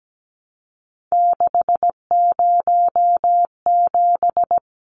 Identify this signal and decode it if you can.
Morse code, heard as a keyed tone, about 17 wpm, 700 Hz. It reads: 607
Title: Numbers - 3-Digit 17wpm